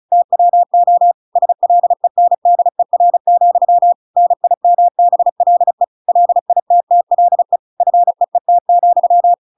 Morse code, heard as a keyed tone, about 34 wpm, 700 Hz. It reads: TWO SLENDER, NIMBLE LITTLE FEET,
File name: Pinocchio / Pinocchio ch8 35 35 ik1hge